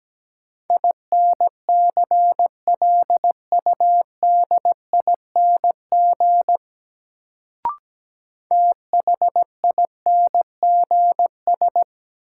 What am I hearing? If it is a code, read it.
Morse code, heard as a keyed tone, about 17 wpm, 700 Hz. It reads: INCLUDING THINGS
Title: Sentences from Top 1000 Words - Review Part 4 - New Word Drill - 17wpm